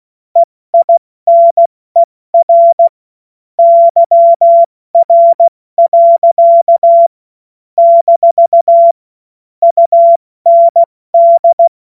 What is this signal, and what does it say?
Morse code, heard as a keyed tone, about 16 wpm, 680 Hz. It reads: EINER YR. - UND
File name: Morsekurs / Witze / Witz 12 März 2024 70 F80